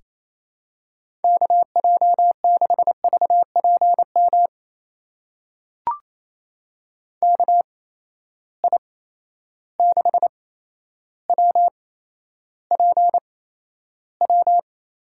Morse code, heard as a keyed tone, about 28 wpm, spaced as for 12 wpm, 700 Hz. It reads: KJ6VPM KI6WPW
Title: USA Call Signs - 6 Characters - 8x Character Spacing - 28wpm